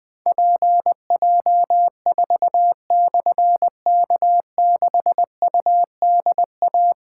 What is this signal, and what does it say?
Morse code, heard as a keyed tone, about 20 wpm, 700 Hz. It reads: PJ4/K6UDA
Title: Difficult Call Signs 20wpm